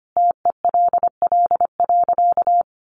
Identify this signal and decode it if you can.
Morse code, heard as a keyed tone, about 25 wpm, 700 Hz. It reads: TELL.
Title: A Sound Of Thunder 25wpm